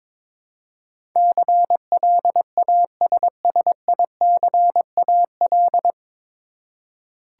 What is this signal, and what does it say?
Morse code, heard as a keyed tone, about 22 wpm, 700 Hz. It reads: CLASSICAL
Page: Morse Code Ninja practice 22wpm